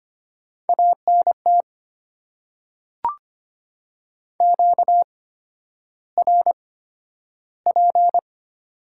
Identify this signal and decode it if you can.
Morse code, heard as a keyed tone, about 25 wpm, spaced as for 11 wpm, 690 Hz. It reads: ANT QRP
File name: Top 90 QSO Elements - 8x Character Spacing - 25wpm